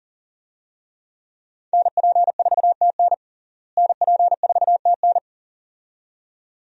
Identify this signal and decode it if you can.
Morse code, heard as a keyed tone, about 40 wpm, 700 Hz. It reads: NP4TN NP4TN
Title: USA Call Signs - 5 Characters - Repeated 2x - 40wpm